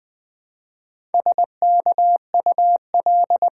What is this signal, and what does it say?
Morse code, heard as a keyed tone, about 20 wpm, 700 Hz. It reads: SKUL
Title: Build A Word - 1 Letter At A Time - 5-Letter Words - 20wpm